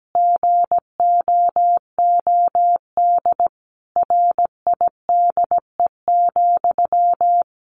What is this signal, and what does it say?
Morse code, heard as a keyed tone, about 17 wpm, 700 Hz. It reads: GOOD RIDE,
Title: Sentences from Top 1200 Words 17wpm